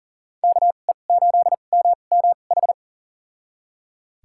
Morse code, heard as a keyed tone, about 40 wpm, spaced as for 29 wpm, 700 Hz. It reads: KE8MMH
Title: USA Call Signs - 6 Characters - 2x Character Spacing - 40wpm